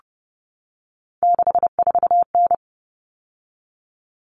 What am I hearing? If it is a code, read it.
Morse code, heard as a keyed tone, about 30 wpm, 700 Hz. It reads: 64N
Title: All Cut Numbers - 3-Digit 30wpm